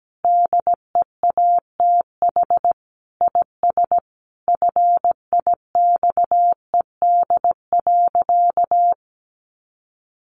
Morse code, heard as a keyed tone, about 17 wpm, 700 Hz. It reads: DEATH IS FIXED.
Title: Sentences from Top 1100 Words 17wpm